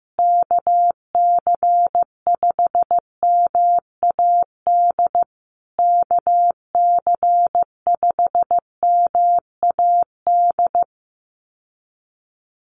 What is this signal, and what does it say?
Morse code, heard as a keyed tone, about 14 wpm, 700 Hz. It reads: KC5MAD KC5MAD
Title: USA Call Signs Repeated 3x 15wpm